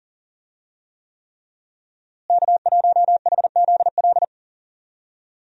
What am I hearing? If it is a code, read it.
Morse code, heard as a keyed tone, about 39 wpm, 700 Hz. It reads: K1HZL
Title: USA Call Signs - 5 Characters - Repeated 3x - 40wpm